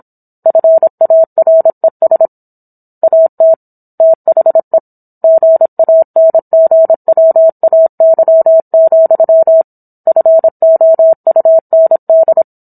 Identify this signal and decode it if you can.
Morse code, heard as a keyed tone, about 27 wpm, 650 Hz. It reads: FARES AT THE GANGWAY, FOUND